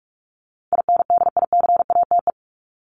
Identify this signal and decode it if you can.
Morse code, heard as a keyed tone, about 44 wpm, 700 Hz. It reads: INDICATE
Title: Top 1251-1500 Words 45wpm